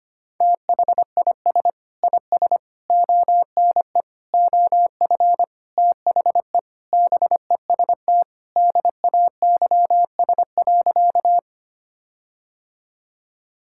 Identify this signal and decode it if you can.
Morse code, heard as a keyed tone, about 25 wpm, 700 Hz. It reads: THIS IS ONE OF THE BEST DAYS.
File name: practice / 25wpm / Sentences with 4-Word Phrases - 25wpm